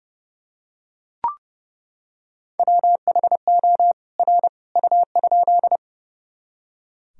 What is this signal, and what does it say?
Morse code, heard as a keyed tone, about 32 wpm, 700 Hz. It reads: WHO R U?
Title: Sentences from Top 100 Words and Abbreviations 30wpm